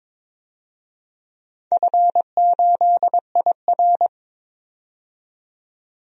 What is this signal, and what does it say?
Morse code, heard as a keyed tone, about 22 wpm, 700 Hz. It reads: F8IR
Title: DXCC-100 Call Signs - 4 Characters - 22wpm